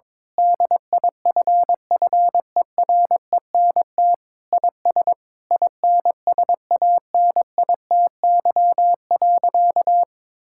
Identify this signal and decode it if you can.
Morse code, heard as a keyed tone, about 22 wpm, 700 Hz. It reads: DIFFERENT IS INSANITY.
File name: practice / 22wpm / Sentences with 8-Word Phrases - 22wpm